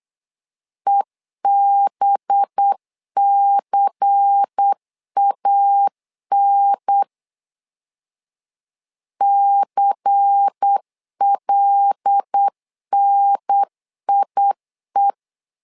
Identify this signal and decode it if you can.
Morse code, heard as a keyed tone, about 8 wpm, 800 Hz. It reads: EBCAN CLNIE
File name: LEZIONI / 5 LBVC / 5 LBVC 8wpm 2935